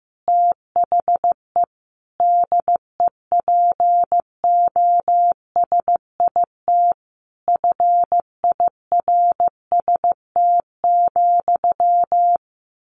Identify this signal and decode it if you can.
Morse code, heard as a keyed tone, about 15 wpm, 700 Hz. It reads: THE DEPOSIT FIRST,
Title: Sentences from Top 1300 Words 15wpm